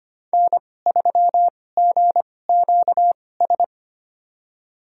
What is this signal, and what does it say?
Morse code, heard as a keyed tone, about 25 wpm, spaced as for 18 wpm, 700 Hz. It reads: N3GQS
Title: USA Call Signs - 5 Characters - 2x Character Spacing - 25wpm